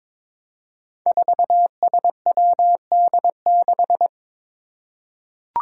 Morse code, heard as a keyed tone, about 22 wpm, 700 Hz. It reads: 4SWD6
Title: Five Challenging Letter-Number Combinations 22wpm